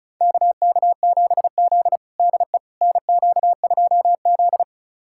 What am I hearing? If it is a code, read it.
Morse code, heard as a keyed tone, about 34 wpm, 700 Hz. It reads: KK7Z DE NQ2Z
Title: USA Callsign DE Callsign - 4 Characters - 35wpm